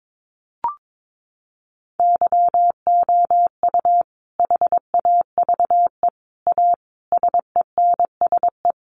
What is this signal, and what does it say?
Morse code, heard as a keyed tone, about 22 wpm, 700 Hz. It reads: YOU HAVE A SENSE
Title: Sentences from Top 700 Words 22wpm